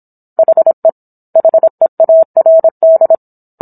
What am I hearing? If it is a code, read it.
Morse code, heard as a keyed tone, about 27 wpm, 650 Hz. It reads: HE HEARD